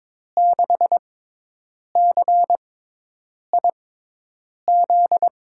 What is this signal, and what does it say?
Morse code, heard as a keyed tone, about 21 wpm, spaced as for 8 wpm, 700 Hz. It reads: 6CIZ
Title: USA Call Signs - 6 Characters - 6x Character Spacing - 22wpm